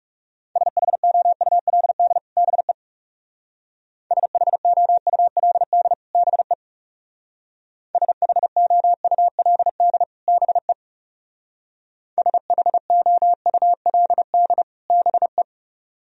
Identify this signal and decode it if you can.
Morse code, heard as a keyed tone, about 37 wpm, 700 Hz. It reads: SHOULD BE SHOULD BE SHOULD BE SHOULD BE
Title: Speed-Racing - Sets of 2 Words 30wpm